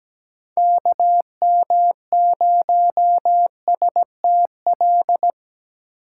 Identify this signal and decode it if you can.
Morse code, heard as a keyed tone, about 17 wpm, 700 Hz. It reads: KM0STL
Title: USA Call Signs - 4x Character Spacing - 17wpm